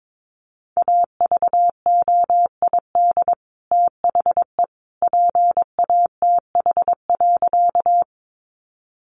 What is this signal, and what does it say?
Morse code, heard as a keyed tone, about 22 wpm, 700 Hz. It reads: AVOID THE PATH.
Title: Sentences from Top 1400 Words 22wpm